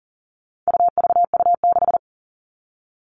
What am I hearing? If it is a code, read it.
Morse code, heard as a keyed tone, about 40 wpm, 700 Hz. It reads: UVU6
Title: All Cut Numbers - 4-Digit 40wpm